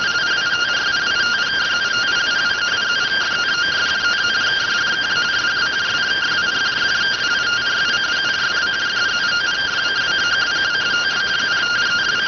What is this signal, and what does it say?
Two-tone teleprinter signal, about 170 Hz shift.